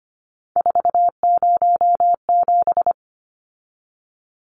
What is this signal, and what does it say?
Morse code, heard as a keyed tone, about 25 wpm, 700 Hz. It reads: 407
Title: Numbers - 3-Digit 25wpm